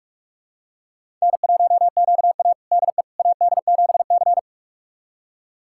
Morse code, heard as a keyed tone, about 47 wpm, 700 Hz. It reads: N1QA DE AD7C
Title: USA Callsign DE Callsign - 4 Characters - 45wpm